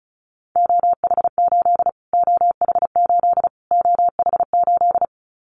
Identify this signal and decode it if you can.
Morse code, heard as a keyed tone, about 34 wpm, 700 Hz. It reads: OH8 OH8 OH8
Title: Callsign Prefixes for Top 25 DXCC - Repeated 3x - 35wpm